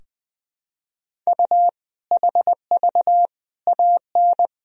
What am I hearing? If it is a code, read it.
Morse code, heard as a keyed tone, about 20 wpm, 700 Hz. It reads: U HV AN
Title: Sentences from Top 400 Words and Abbreviations 20wpm